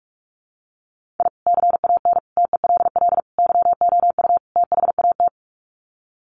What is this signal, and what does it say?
Morse code, heard as a keyed tone, about 44 wpm, 700 Hz. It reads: I CAN TELL YOU THAT
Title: Sets of 5 Words 45wpm